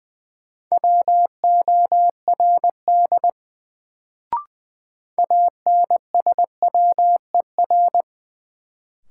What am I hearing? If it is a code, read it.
Morse code, heard as a keyed tone, about 20 wpm, 700 Hz. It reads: WORD ANSWER
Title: Sentences from Top 300 Words - New Word Drill - 20wpm